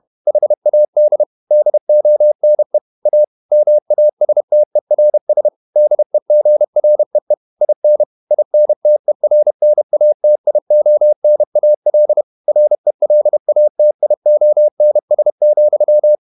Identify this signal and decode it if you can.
Morse code, heard as a keyed tone, about 31 wpm, 600 Hz. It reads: HAD DONE A MASTERS DEGREE IN INTERNATIONAL RELATIONS,
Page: Never 31wpm | CW med Gnister